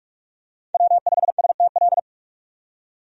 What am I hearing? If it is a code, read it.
Morse code, heard as a keyed tone, about 47 wpm, 700 Hz. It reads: W5STL
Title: USA Call Signs - 5 Characters - 45wpm